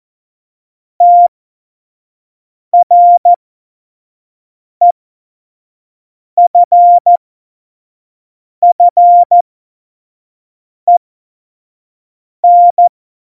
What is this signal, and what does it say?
Morse code, heard as a keyed tone, about 14 wpm, spaced as for 5 wpm, 700 Hz. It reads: TREFFEN
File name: Morsekurs / Witze / Witz 24.Juli 2018, 25 F70